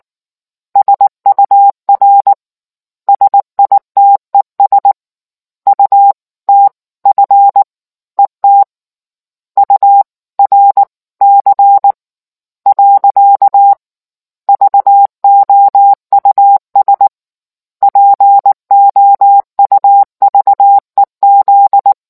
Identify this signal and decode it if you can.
Morse code, heard as a keyed tone, about 19 wpm, spaced as for 17 wpm, 800 Hz. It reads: SUR SITES UTF ET URC. VOUS POUVEZ